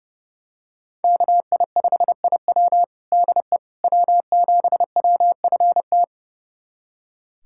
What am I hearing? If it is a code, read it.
Morse code, heard as a keyed tone, about 30 wpm, 700 Hz. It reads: KI5IW DE W7WFT
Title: USA Callsign DE Callsign 30wpm